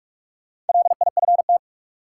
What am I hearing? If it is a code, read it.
Morse code, heard as a keyed tone, about 43 wpm, 700 Hz. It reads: LIFT
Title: Sentences from Top 1000 Words - New Word Drill - 45wpm